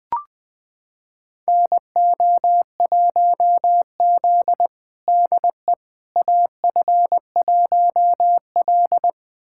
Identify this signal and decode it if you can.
Morse code, heard as a keyed tone, about 20 wpm, 700 Hz. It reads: NO1Z DE AF1L
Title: USA Callsign DE Callsign - 4 Characters - 20wpm